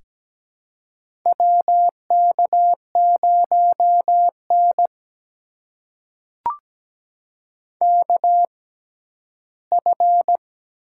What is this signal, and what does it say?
Morse code, heard as a keyed tone, about 17 wpm, spaced as for 7 wpm, 700 Hz. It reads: WK0NKF
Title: USA Call Signs - 6x Character Spacing - 17wpm